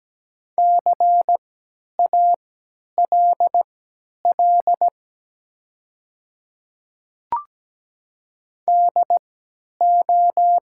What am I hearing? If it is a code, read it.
Morse code, heard as a keyed tone, about 17 wpm, spaced as for 10 wpm, 700 Hz. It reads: CALL DO
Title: 3x Character Spacing - Top 100 Words - 17wpm